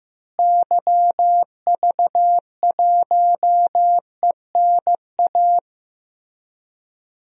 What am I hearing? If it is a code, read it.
Morse code, heard as a keyed tone, about 15 wpm, 700 Hz. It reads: YV1ENA
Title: DXCC-100 Call Signs - 6 Characters - Repeated 2x - 15wpm